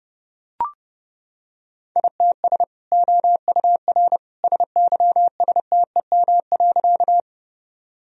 Morse code, heard as a keyed tone, about 30 wpm, 700 Hz. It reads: ITS OUR SYSTEM.
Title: Sentences from Top 1000 Words - Review Part 3 - 30wpm